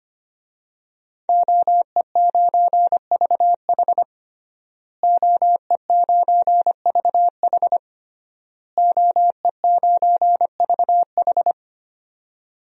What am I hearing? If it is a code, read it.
Morse code, heard as a keyed tone, about 25 wpm, 700 Hz. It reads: OE9VH OE9VH OE9VH
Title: DXCC-100 Call Signs - 5 Characters - Repeated 3x - 25wpm